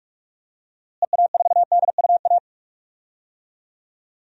Morse code, heard as a keyed tone, about 46 wpm, 700 Hz. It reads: EA4DUA
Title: DXCC-100 Call Signs - 6 Characters - Repeated 3x - 45wpm